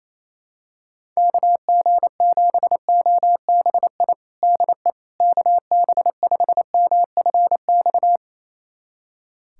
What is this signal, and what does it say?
Morse code, heard as a keyed tone, about 28 wpm, 700 Hz. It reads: KG7OBI DE KB5MFX
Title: USA Callsign DE Callsign - 6 Characters - 28wpm